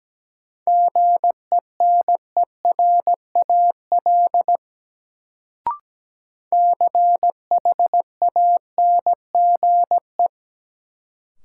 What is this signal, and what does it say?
Morse code, heard as a keyed tone, about 17 wpm, 700 Hz. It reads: GENERAL CHANGE
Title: Sentences from Top 600 Words - New Word Drill - 17wpm